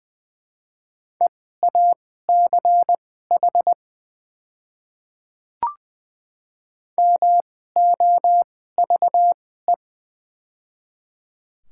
Morse code, heard as a keyed tone, about 20 wpm, spaced as for 14 wpm, 700 Hz. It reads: EACH MOVE
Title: 2x Character Spacing - Top 200 Words - 20wpm